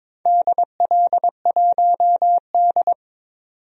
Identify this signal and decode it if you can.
Morse code, heard as a keyed tone, about 22 wpm, 700 Hz. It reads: DL1D
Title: DXCC-100 Call Signs - 4 Characters - 22wpm